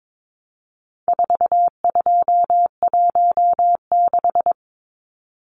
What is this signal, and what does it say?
Morse code, heard as a keyed tone, about 22 wpm, 700 Hz. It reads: 4216
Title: Numbers - 4-Digit 22wpm